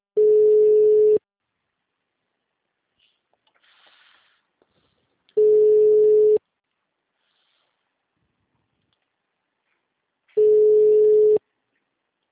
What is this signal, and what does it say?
Ringback tone (Europe, 425 Hz).